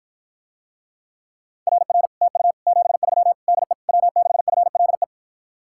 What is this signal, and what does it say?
Morse code, heard as a keyed tone, about 54 wpm, 700 Hz. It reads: RR TU 73 DE W6FLE